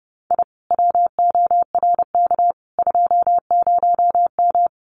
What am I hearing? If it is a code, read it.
Morse code, heard as a keyed tone, about 30 wpm, 700 Hz. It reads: I WORK 20M
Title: Common QSO Phrases Encore 30wpm